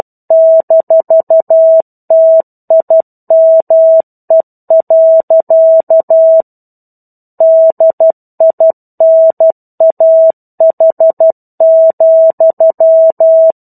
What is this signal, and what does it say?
Morse code, heard as a keyed tone, about 12 wpm, 650 Hz. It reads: -TIME. DINAH,
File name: 01-Chapter-01 12 WPM